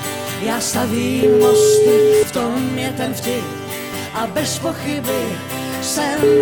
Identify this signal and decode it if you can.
Ringback tone (Europe, 425 Hz), with noise.